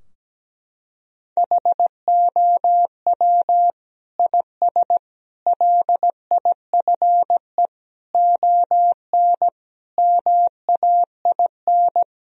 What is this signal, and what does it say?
Morse code, heard as a keyed tone, about 17 wpm, 700 Hz. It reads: HOW IS LIFE ON MAIN
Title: Sentences from Top 1000 Words - Review Part 2 - 17wpm